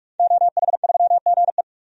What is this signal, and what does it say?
Morse code, heard as a keyed tone, about 41 wpm, 700 Hz. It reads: OH3GE
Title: DXCC-100 Call Signs - 5 Characters - 45wpm